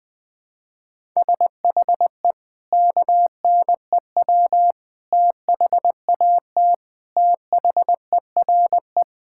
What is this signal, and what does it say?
Morse code, heard as a keyed tone, about 20 wpm, 700 Hz. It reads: SHE KNEW THAT THERE
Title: Sentences with 4-Word Phrases - 20wpm